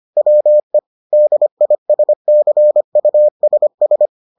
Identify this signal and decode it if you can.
Morse code, heard as a keyed tone, about 25 wpm, 600 Hz. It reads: WE DISCUSS